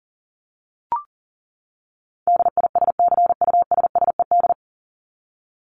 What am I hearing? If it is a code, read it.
Morse code, heard as a keyed tone, about 40 wpm, 700 Hz. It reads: DISCUSSED
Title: ED Suffix - Top 100 Words with Base Words 40wpm